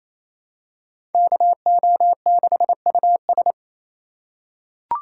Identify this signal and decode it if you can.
Morse code, heard as a keyed tone, about 28 wpm, 700 Hz. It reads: KO6US
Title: USA Call Signs - 2x Character Spacing - 28wpm